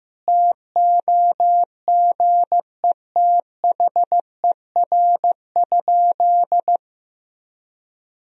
Morse code, heard as a keyed tone, about 15 wpm, 700 Hz. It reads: TOGETHER?
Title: Sentences from Top 1000 Words - Review Part 2 - 15wpm